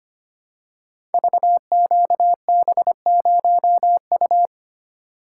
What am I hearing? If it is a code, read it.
Morse code, heard as a keyed tone, about 25 wpm, 700 Hz. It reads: VQB0U
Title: Code Groups - 5 Letter:Numbers - 25wpm